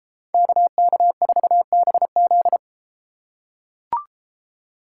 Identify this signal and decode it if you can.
Morse code, heard as a keyed tone, about 32 wpm, 700 Hz. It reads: KK4BZ